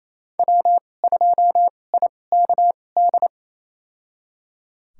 Morse code, heard as a keyed tone, about 28 wpm, spaced as for 20 wpm, 700 Hz. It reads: W2IKD
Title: USA Call Signs - 5 Characters - 2x Character Spacing - 28wpm